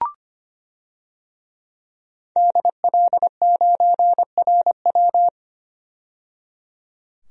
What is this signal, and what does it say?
Morse code, heard as a keyed tone, about 25 wpm, 700 Hz. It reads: DL9RW
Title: DXCC-100 Call Signs - 5 Characters - 25wpm